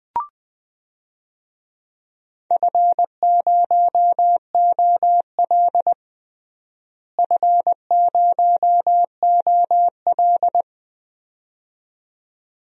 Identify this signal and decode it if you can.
Morse code, heard as a keyed tone, about 21 wpm, 700 Hz. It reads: F0OL F0OL
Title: DXCC-100 Call Signs - 4 Characters - Repeated 2x - 20wpm